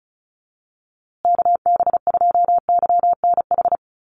Morse code, heard as a keyed tone, about 34 wpm, 700 Hz. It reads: KB2YNH
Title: USA Call Signs Repeated 3x 35wpm